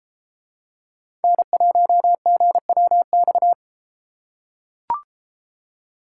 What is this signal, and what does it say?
Morse code, heard as a keyed tone, about 31 wpm, 700 Hz. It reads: N1GWX